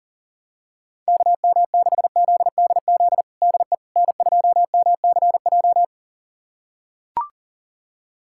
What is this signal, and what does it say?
Morse code, heard as a keyed tone, about 40 wpm, 700 Hz. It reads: KM6ZDZ DE N2MCJ
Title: USA Callsign DE Callsign 40wpm